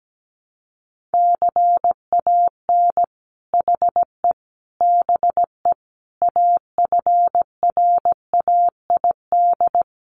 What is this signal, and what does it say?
Morse code, heard as a keyed tone, about 17 wpm, 700 Hz. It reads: CAN HE BE AFRAID
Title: Sentences from Top 1500 Words 17wpm